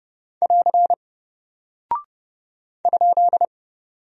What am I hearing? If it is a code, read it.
Morse code, heard as a keyed tone, about 30 wpm, 700 Hz. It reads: <AR> ?
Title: Single Character (Letter:Number:Punctuation:Prosign) Alt 30wpm